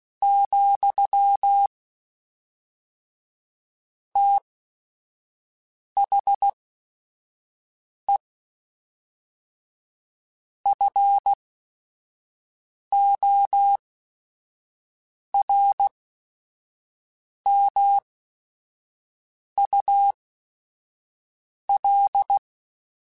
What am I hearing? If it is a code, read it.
Morse code, heard as a keyed tone, about 16 wpm, spaced as for 5 wpm, 800 Hz. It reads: , THE FORMUL
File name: Technical Subjects / Technical Subjects-mp3s / XL @5